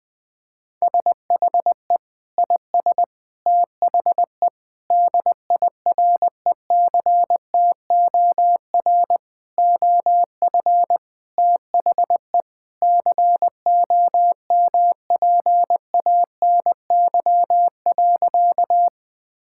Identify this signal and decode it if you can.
Morse code, heard as a keyed tone, about 20 wpm, 700 Hz. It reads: SHE IS THE DIRECTOR OF THE COMPANY.
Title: Sentences with 3-Word Phrases - 20wpm